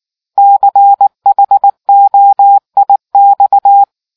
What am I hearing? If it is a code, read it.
Morse code, heard as a keyed tone, about 19 wpm, 800 Hz. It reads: CHOIX